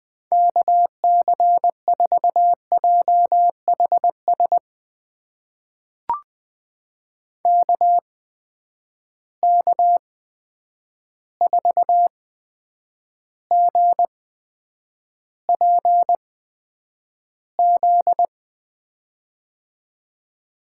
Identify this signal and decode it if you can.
Morse code, heard as a keyed tone, about 20 wpm, spaced as for 9 wpm, 700 Hz. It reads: KC4JHS KK4GPZ
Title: USA Call Signs - 6 Characters - 8x Character Spacing - 20wpm